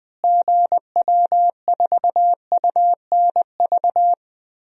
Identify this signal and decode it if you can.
Morse code, heard as a keyed tone, about 21 wpm, 700 Hz. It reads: GW4UNV
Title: DXCC-100 Call Signs - 6 Characters - Repeated 3x - 20wpm